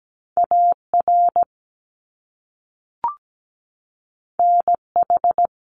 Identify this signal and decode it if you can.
Morse code, heard as a keyed tone, about 17 wpm, 700 Hz. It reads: AR NH
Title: US States 17wpm